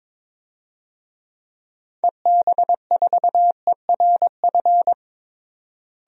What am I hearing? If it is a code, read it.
Morse code, heard as a keyed tone, about 21 wpm, 700 Hz. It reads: EB4ERF
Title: DXCC-100 Call Signs - 6 Characters - Repeated 2x - 22wpm